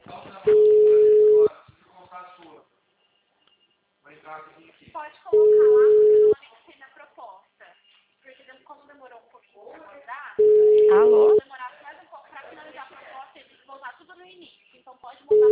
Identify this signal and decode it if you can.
Ringback tone (Europe, 425 Hz), with speech.